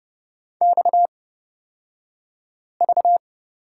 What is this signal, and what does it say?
Morse code, heard as a keyed tone, about 31 wpm, spaced as for 5 wpm, 700 Hz. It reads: XV
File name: Lesson 38d - Letter q - Two Characters 30wpm